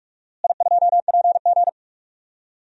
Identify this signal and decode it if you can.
Morse code, heard as a keyed tone, about 46 wpm, 700 Hz. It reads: I2PG
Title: DXCC-100 Call Signs - 4 Characters - Repeated 3x - 45wpm